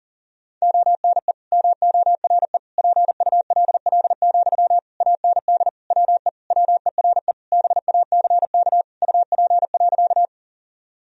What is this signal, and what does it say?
Morse code, heard as a keyed tone, about 37 wpm, 700 Hz. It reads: ONE MORE PULL, AND WE WERE BACK UP.
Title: Sentences from Top 1000 Words - Review Part 2 - 40wpm